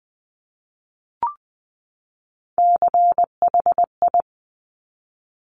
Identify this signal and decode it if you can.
Morse code, heard as a keyed tone, about 20 wpm, 700 Hz. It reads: CHI
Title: Top Three Letter Combinations in Words 20wpm